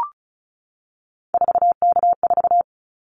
Morse code, heard as a keyed tone, about 36 wpm, 700 Hz. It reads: 4K4
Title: Callsign Prefixes for Top 100 DXCC - 35wpm